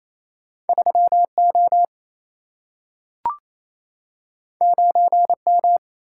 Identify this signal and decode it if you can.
Morse code, heard as a keyed tone, about 28 wpm, 700 Hz. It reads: 3 O 9 M
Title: Single-Letter Single-Number 28wpm